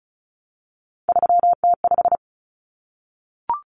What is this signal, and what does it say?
Morse code, heard as a keyed tone, about 34 wpm, 700 Hz. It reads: 3T5
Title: Common Cut Numbers - 3-Digit 35wpm